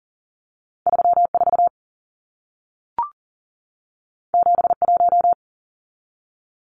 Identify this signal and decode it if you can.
Morse code, heard as a keyed tone, about 40 wpm, 700 Hz. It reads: 34 71
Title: Numbers - 2-Digit 40wpm